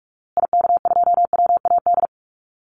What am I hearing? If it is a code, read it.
Morse code, heard as a keyed tone, about 44 wpm, 700 Hz. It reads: IK2WAD
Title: Call Signs Repeated 2x - 45wpm